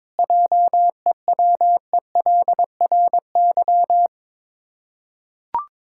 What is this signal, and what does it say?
Morse code, heard as a keyed tone, about 22 wpm, 690 Hz. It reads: JEWELRY
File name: Difficult Words with Hints - 22wpm